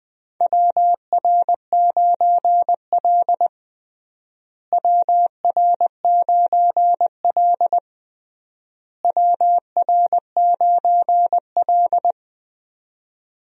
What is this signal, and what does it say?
Morse code, heard as a keyed tone, about 20 wpm, 700 Hz. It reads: WR9L WR9L WR9L
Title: USA Call Signs - 4 Characters - Repeated 3x - 20wpm